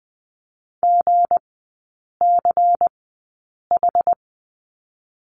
Morse code, heard as a keyed tone, about 20 wpm, spaced as for 8 wpm, 690 Hz. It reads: GCH
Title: 159 Lesson 26e - Letter g - Three Characters 20wpm